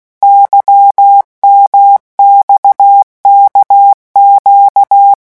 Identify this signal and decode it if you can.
Morse code, heard as a keyed tone, about 16 wpm, 800 Hz. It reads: YMXKQ
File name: morse mp3 / revision1-vitesse 15 mots